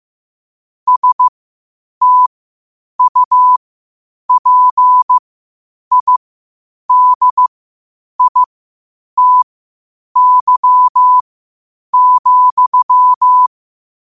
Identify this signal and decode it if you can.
Morse code, heard as a keyed tone, about 15 wpm, spaced as for 8 wpm, 1000 Hz. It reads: STUPIDITY,
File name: qotdfiles / 8wpm